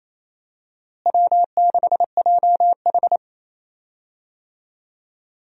Morse code, heard as a keyed tone, about 27 wpm, 700 Hz. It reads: W6JH
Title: USA Call Signs - 4 Characters - Repeated 3x - 28wpm